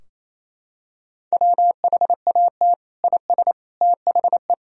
Morse code, heard as a keyed tone, about 28 wpm, 700 Hz. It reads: WHAT IS THE
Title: Sentences from Top 1000 Words - Review Part 4 - 28wpm